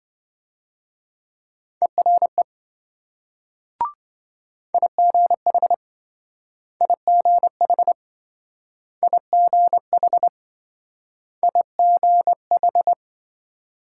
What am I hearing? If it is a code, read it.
Morse code, heard as a keyed tone, about 26 wpm, 700 Hz. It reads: ERE IGH IGH IGH IGH